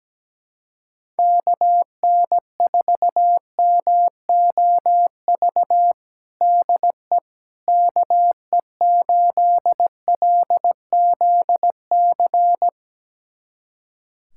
Morse code, heard as a keyed tone, about 17 wpm, 700 Hz. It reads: KN4MOV DE KE8LZC
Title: USA Callsign DE Callsign - 6 Characters - 17wpm